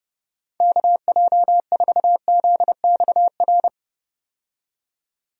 Morse code, heard as a keyed tone, about 30 wpm, 700 Hz. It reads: KJ4ZXR
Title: USA Call Signs - 6 Characters - 30wpm